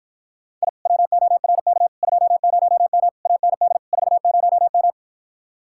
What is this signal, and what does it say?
Morse code, heard as a keyed tone, about 53 wpm, 700 Hz. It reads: I WORK 20M AND 40M